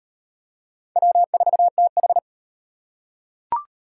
Morse code, heard as a keyed tone, about 34 wpm, 700 Hz. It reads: W4TH